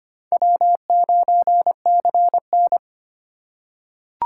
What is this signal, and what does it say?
Morse code, heard as a keyed tone, about 25 wpm, 700 Hz. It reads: W9CN
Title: USA Call Signs - 4 Characters - 2x Character Spacing - 25wpm